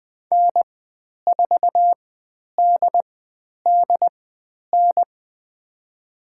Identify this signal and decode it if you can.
Morse code, heard as a keyed tone, about 20 wpm, spaced as for 10 wpm, 700 Hz. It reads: N4DDN
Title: 102 Lesson 17g - Letter u - Call Signs 20wpm